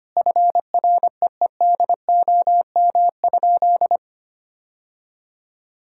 Morse code, heard as a keyed tone, about 25 wpm, 700 Hz. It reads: FREEDOM?
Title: Sentences from Top 1000 Words - Review Part 5 - 25wpm